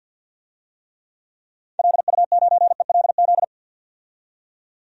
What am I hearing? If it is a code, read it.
Morse code, heard as a keyed tone, about 51 wpm, 700 Hz. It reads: LU9ELZ
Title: DXCC-100 Call Signs - 6 Characters - 50wpm